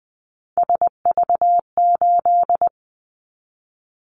Morse code, heard as a keyed tone, about 20 wpm, 700 Hz. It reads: SV8
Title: Callsign Prefixes for Top 50 DXCC - 20wpm